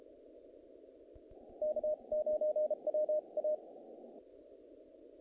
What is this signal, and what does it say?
Morse code, heard as a keyed tone, about 32 wpm, spaced as for 28 wpm, 610 Hz. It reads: K9WA
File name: contests / naqp-cw-k5tr-op-k5pi- -1800